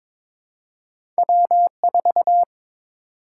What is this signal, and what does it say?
Morse code, heard as a keyed tone, about 22 wpm, 700 Hz. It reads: W4
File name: practice / 22wpm / Two Letter-Number Combinations - 22wpm